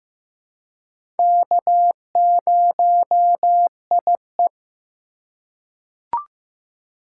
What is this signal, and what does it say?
Morse code, heard as a keyed tone, about 15 wpm, 700 Hz. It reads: K0IE
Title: USA Call Signs - 4 Characters - 3x Character Spacing - 15wpm